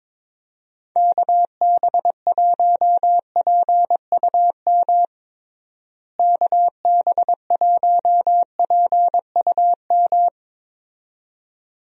Morse code, heard as a keyed tone, about 23 wpm, 700 Hz. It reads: KB1PUM KB1PUM
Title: USA Call Signs - 6 Characters - Repeated 2x - 22wpm